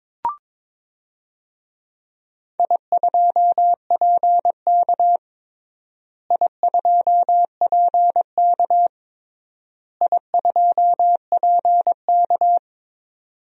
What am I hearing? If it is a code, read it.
Morse code, heard as a keyed tone, about 22 wpm, 700 Hz. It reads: I2PK I2PK I2PK
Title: DXCC-100 Call Signs - 4 Characters - Repeated 3x - 22wpm